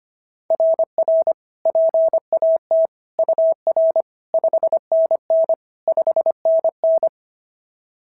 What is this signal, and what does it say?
Morse code, heard as a keyed tone, about 25 wpm, 690 Hz. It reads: RR PAT UR 5NN 5NN
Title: SOTA QSO Practice 25wpm